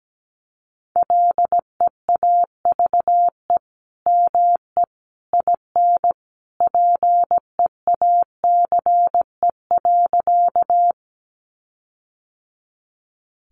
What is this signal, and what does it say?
Morse code, heard as a keyed tone, about 17 wpm, 700 Hz. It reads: LEAVE ME IN PEACE.
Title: Sentences from Top 1500 Words 17wpm